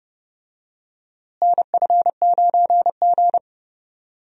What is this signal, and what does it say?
Morse code, heard as a keyed tone, about 30 wpm, 700 Hz. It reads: NF9G
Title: USA Call Signs - 4 Characters - Repeated 3x - 30wpm